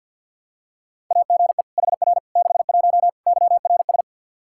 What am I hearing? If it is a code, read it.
Morse code, heard as a keyed tone, about 51 wpm, 700 Hz. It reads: AGE HR 61 YRS
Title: Practice QSO - Exchange 3 - More Info - 50wpm